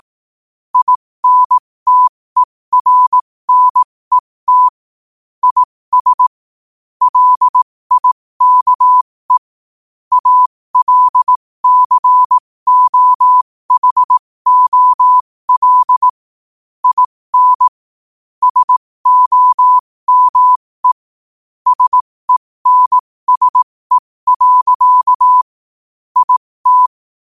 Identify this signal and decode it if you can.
Morse code, heard as a keyed tone, about 18 wpm, spaced as for 15 wpm, 1000 Hz. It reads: INTERNET IS LIKE ALCOHOL IN SOME SENSE. IT
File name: qotdfiles / 15wpm